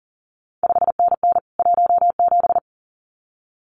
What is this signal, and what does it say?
Morse code, heard as a keyed tone, about 38 wpm, 700 Hz. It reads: 5NN 17
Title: CQ WW DX Contest Exchange Encore - 40wpm